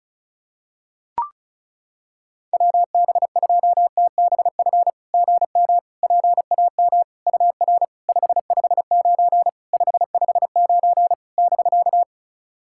Morse code, heard as a keyed tone, about 36 wpm, 690 Hz. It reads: WB2TBF GM PAM UR 559 559 <BK>
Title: SOTA QSO Practice 35wpm